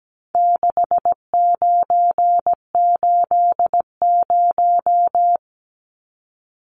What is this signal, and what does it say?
Morse code, heard as a keyed tone, about 17 wpm, 700 Hz. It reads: 6980
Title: Numbers - 4-Digit 17wpm